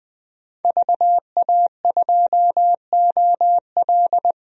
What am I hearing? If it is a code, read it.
Morse code, heard as a keyed tone, about 20 wpm, 700 Hz. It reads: VA2OL
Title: DXCC-100 Call Signs - 5 Characters - 20wpm